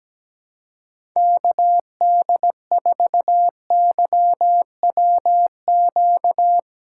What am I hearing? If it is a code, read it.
Morse code, heard as a keyed tone, about 17 wpm, 700 Hz. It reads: KD4YWQ
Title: USA Call Signs - 6 Characters - 4x Character Spacing - 17wpm